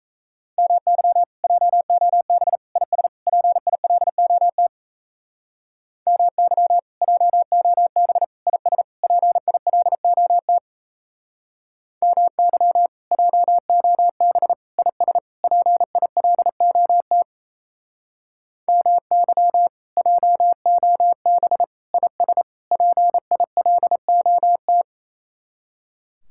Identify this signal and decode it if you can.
Morse code, heard as a keyed tone, about 37 wpm, 700 Hz. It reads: MY JOB IS PILOT MY JOB IS PILOT MY JOB IS PILOT MY JOB IS PILOT